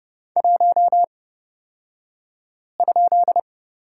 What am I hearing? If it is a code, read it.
Morse code, heard as a keyed tone, about 30 wpm, spaced as for 5 wpm, 700 Hz. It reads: 1?
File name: Lesson 38d - Letter q - Two Characters 30wpm